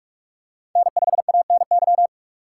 Morse code, heard as a keyed tone, about 46 wpm, 700 Hz. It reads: N5ANY
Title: USA Call Signs - 3x Character Spacing - 45wpm